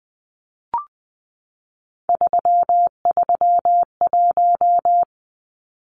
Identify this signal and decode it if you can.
Morse code, heard as a keyed tone, about 20 wpm, 700 Hz. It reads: 331
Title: Numbers - 3-Digit 20wpm